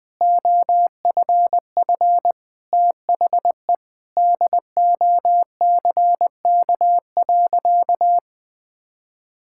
Morse code, heard as a keyed tone, about 20 wpm, 700 Hz. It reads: OFF THE DOCK.
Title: Sentences with 2-Word Phrases - Spoken Beginning - 20wpm